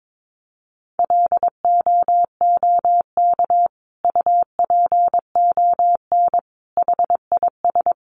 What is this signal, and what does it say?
Morse code, heard as a keyed tone, about 22 wpm, 700 Hz. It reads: LOOK UPON HIS
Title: Sentences from Top 1100 Words 22wpm